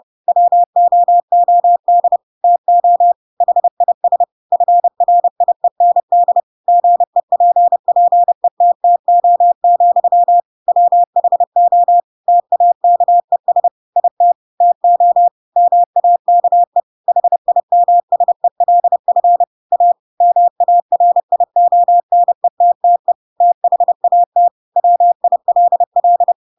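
Morse code, heard as a keyed tone, about 30 wpm, 700 Hz. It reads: WOOD TO HIS FRIEND GEPPETTO, WHO TAKES IT TO MAKE HIMSELF A MARIONETTE THAT WILL
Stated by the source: Chapter 2 of The Adventures of Pinocchio read as a book in Morse code (CW) at 30 wpm, by ik1hge.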